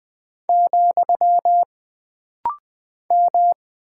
Morse code, heard as a keyed tone, about 20 wpm, 700 Hz. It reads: , M
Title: Single Character (Letter:Number:Punctuation:Prosign) - Rapid-Fire - 20wpm